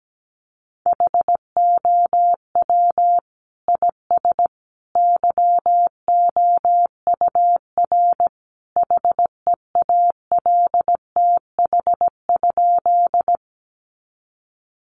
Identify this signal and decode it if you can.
Morse code, heard as a keyed tone, about 17 wpm, 700 Hz. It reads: HOW IS YOUR HEALTH?
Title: Sentences from Top 600 Words 17wpm